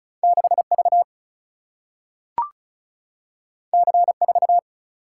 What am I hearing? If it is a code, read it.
Morse code, heard as a keyed tone, about 36 wpm, 700 Hz. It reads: 6 V C 4
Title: Single-Letter Single-Number 35wpm